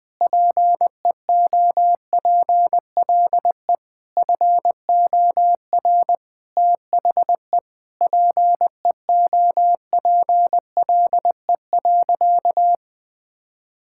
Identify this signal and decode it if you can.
Morse code, heard as a keyed tone, about 20 wpm, 700 Hz. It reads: PEOPLE FOR THE PEOPLE.
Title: Sentences with 9-Word Phrases - 20wpm